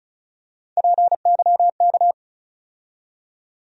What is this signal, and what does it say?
Morse code, heard as a keyed tone, about 34 wpm, 700 Hz. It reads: PYK
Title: USA Call Sign Postfixes - 3 Letters 35wpm